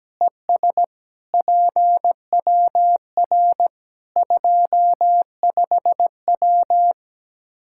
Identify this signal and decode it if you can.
Morse code, heard as a keyed tone, about 17 wpm, 700 Hz. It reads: ES PWR 25W
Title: Practice QSO - Exchange 2 - Info - 17wpm